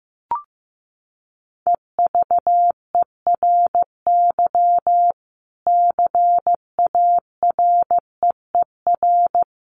Morse code, heard as a keyed tone, about 15 wpm, 700 Hz. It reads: EVERY CAREER
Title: Sentences from Top 1000 Words 15wpm